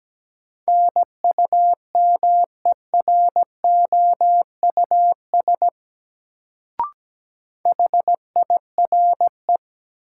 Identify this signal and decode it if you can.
Morse code, heard as a keyed tone, about 17 wpm, 700 Hz. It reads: NUMEROUS HIRE
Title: Sentences from Top 1400 Words - New Word Drill Alt - 17wpm